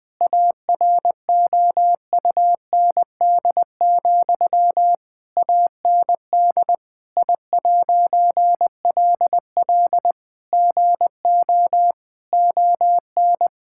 Morse code, heard as a keyed tone, about 20 wpm, 700 Hz. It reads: AROUND, AND I'LL GO ON
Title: All-Age Appropriate Jokes - 20wpm